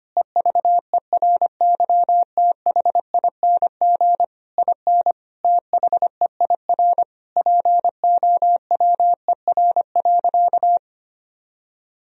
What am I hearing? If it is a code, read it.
Morse code, heard as a keyed tone, about 25 wpm, 700 Hz. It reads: EVERYTHING IN THEIR POWER.
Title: Sentences from Top 1000 Words - Review Part 2 - 25wpm